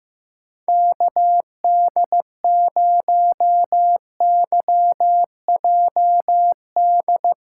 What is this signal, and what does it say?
Morse code, heard as a keyed tone, about 15 wpm, 700 Hz. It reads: KD0YJD
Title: USA Call Signs - 2x Character Spacing - 15wpm